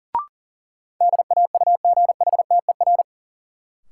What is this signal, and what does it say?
Morse code, heard as a keyed tone, about 40 wpm, 700 Hz. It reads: DAUGHTER
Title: Sentences from Top 1500 Words - New Word Drill - 40wpm